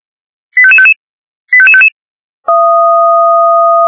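Five-tone selective call (ZVEI), heard as quick runs of five tones, tones of about 70 ms. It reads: 85555 85555
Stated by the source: Five-tone ZVEI selective call 85555, received by SDR.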